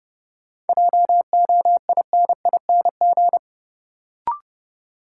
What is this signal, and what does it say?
Morse code, heard as a keyed tone, about 32 wpm, 700 Hz. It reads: JOINING
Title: Sets of 5 Words - New Word Drill - 30wpm